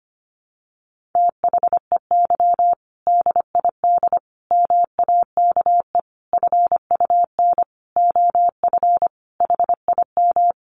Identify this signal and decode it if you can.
Morse code, heard as a keyed tone, about 25 wpm, 700 Hz. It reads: THEY DID MAKE FUN OF HIM
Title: Sentences from Top 1300 Words 25wpm